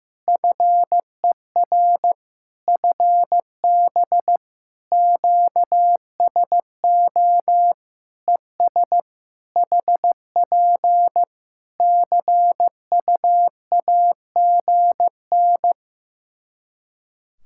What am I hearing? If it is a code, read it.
Morse code, heard as a keyed tone, about 15 wpm, 700 Hz. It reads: FER FB QSO ES HP CUAGN
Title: K1USN SST QSO Practice 15wpm